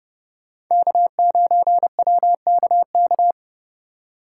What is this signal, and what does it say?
Morse code, heard as a keyed tone, about 39 wpm, 700 Hz. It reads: K9WKK